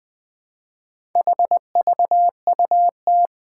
Morse code, heard as a keyed tone, about 20 wpm, 700 Hz. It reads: HVUT
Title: Four Challenging Letter-Number Combinations 20wpm